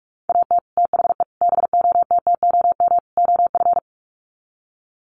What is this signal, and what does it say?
Morse code, heard as a keyed tone, about 44 wpm, 700 Hz. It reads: AT THE BOTTOM OF
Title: Sets of 4 Words 45wpm